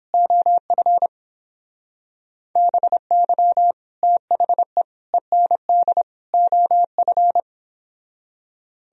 Morse code, heard as a keyed tone, about 29 wpm, 700 Hz. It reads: OF BY THE END OF